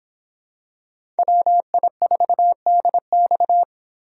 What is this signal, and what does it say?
Morse code, heard as a keyed tone, about 25 wpm, 700 Hz. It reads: WI4DX